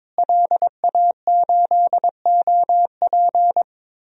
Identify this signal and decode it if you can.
Morse code, heard as a keyed tone, about 22 wpm, 700 Hz. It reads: LA8OP
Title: DXCC-100 Call Signs - 5 Characters - Repeated 3x - 22wpm